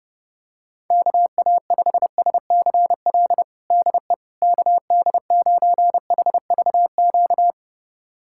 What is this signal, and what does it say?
Morse code, heard as a keyed tone, about 30 wpm, 700 Hz. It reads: KA5SCL DE KD9HVQ
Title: USA Callsign DE Callsign - 6 Characters - 30wpm